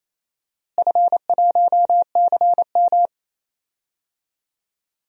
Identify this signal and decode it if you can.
Morse code, heard as a keyed tone, about 28 wpm, 700 Hz. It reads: F1CM
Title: DXCC-100 Call Signs - 4 Characters - Repeated 2x - 28wpm